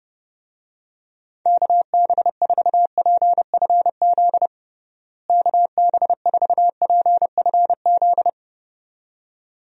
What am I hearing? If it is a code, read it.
Morse code, heard as a keyed tone, about 30 wpm, 700 Hz. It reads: KB4PFZ KB4PFZ
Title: USA Call Signs - 6 Characters - Repeated 2x - 30wpm